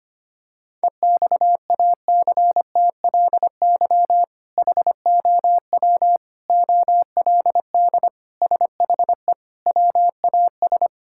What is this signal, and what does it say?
Morse code, heard as a keyed tone, about 25 wpm, 700 Hz. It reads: EXACTLY HOW OLD SHE WAS
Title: Sentences from Top 1600 Words 25wpm